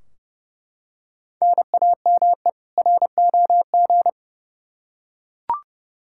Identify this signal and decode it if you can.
Morse code, heard as a keyed tone, about 30 wpm, 700 Hz. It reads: NAME ROG
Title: Practice QSO - Exchange 1 - Beginning - 30wpm